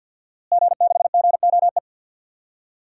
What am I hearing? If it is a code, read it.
Morse code, heard as a keyed tone, about 48 wpm, 700 Hz. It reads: G6GOE
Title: DXCC-100 Call Signs - 5 Characters - Repeated 2x - 50wpm